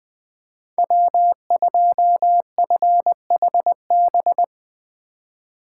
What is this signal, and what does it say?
Morse code, heard as a keyed tone, about 21 wpm, 700 Hz. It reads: W2FHB
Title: USA Call Signs - 5 Characters - 4x Character Spacing - 20wpm